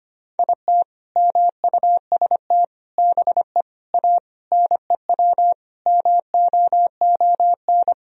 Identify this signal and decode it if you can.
Morse code, heard as a keyed tone, about 25 wpm, 700 Hz. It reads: IT MUST BE A NEW MOON
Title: Sentences from Top 500 Words and Abbreviations 25wpm